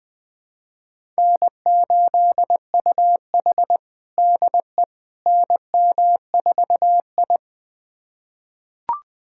Morse code, heard as a keyed tone, about 21 wpm, 700 Hz. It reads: N8UH DE NM4I
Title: USA Callsign DE Callsign - 4 Characters - 20wpm